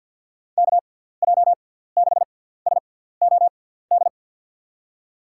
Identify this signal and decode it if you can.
Morse code, heard as a keyed tone, about 50 wpm, spaced as for 17 wpm, 700 Hz. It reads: KJ6SOD
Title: USA Call Signs - 6 Characters - 6x Character Spacing - 50wpm